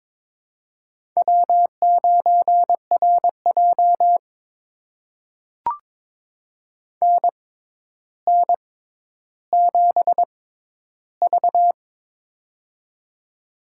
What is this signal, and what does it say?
Morse code, heard as a keyed tone, about 22 wpm, spaced as for 11 wpm, 700 Hz. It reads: W9RJ NN7V
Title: USA Call Signs - 4 Characters - 6x Character Spacing - 22wpm